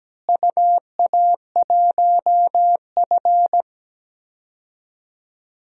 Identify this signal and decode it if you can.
Morse code, heard as a keyed tone, about 17 wpm, 700 Hz. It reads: UA1F
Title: DXCC-100 Call Signs - 4 Characters - Repeated 2x - 17wpm